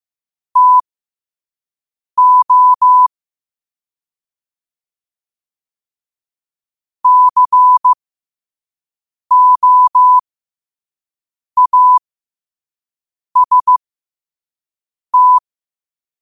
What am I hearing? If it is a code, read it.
Morse code, heard as a keyed tone, about 15 wpm, spaced as for 5 wpm, 1000 Hz. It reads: TO COAST
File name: qotdfiles / 5wpm